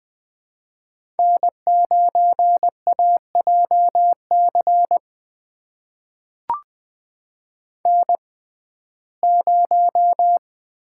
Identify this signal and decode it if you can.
Morse code, heard as a keyed tone, about 20 wpm, spaced as for 14 wpm, 700 Hz. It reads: N9AJC N0
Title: USA Call Signs - 5 Characters - 6x Character Spacing - 20wpm